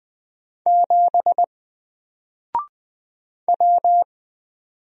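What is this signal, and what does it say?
Morse code, heard as a keyed tone, about 20 wpm, 700 Hz. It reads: 7 W
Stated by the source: Single letter and number Morse drill at 20 wpm, from N4LCD.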